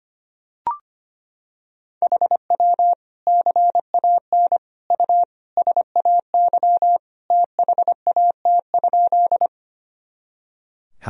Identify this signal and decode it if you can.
Morse code, heard as a keyed tone, about 25 wpm, 700 Hz. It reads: HW CAN U SAY THAT?
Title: Sentences from Top 200 Words and Abbreviations 25wpm